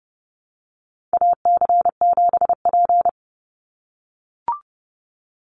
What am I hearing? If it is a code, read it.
Morse code, heard as a keyed tone, about 30 wpm, 700 Hz. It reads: AC7P
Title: NA SOTA Chaser Callsigns 30wpm